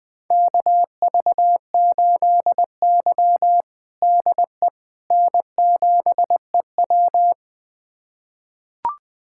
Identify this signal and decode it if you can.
Morse code, heard as a keyed tone, about 20 wpm, 700 Hz. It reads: KV8Y DE N7EW
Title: USA Callsign DE Callsign - 4 Characters - 20wpm